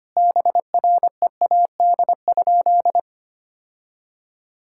Morse code, heard as a keyed tone, about 25 wpm, 700 Hz. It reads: BREAD?
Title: Sentences from Top 1600 Words 25wpm